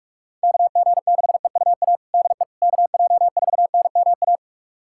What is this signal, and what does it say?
Morse code, heard as a keyed tone, about 46 wpm, 690 Hz. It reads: KG6EUA DE KJ4NMA
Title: USA Callsign DE Callsign - 6 Characters - 45wpm